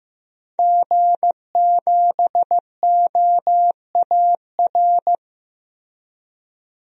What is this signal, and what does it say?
Morse code, heard as a keyed tone, about 15 wpm, 700 Hz. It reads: G7OAR
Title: DXCC-100 Call Signs - 5 Characters - Repeated 3x - 15wpm